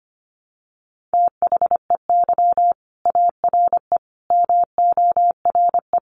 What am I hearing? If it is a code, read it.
Morse code, heard as a keyed tone, about 25 wpm, 700 Hz. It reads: THEY ARE MORE
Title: Sets of 5 Words Encore2 25wpm